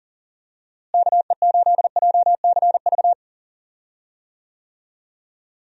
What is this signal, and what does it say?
Morse code, heard as a keyed tone, about 40 wpm, 700 Hz. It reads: KE8JCV
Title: USA Call Signs - 6 Characters - Repeated 2x - 40wpm